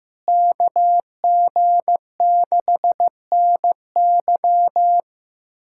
Morse code, heard as a keyed tone, about 15 wpm, 700 Hz. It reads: KG6NY
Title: USA Callsign DE Callsign - 5 Characters - 15wpm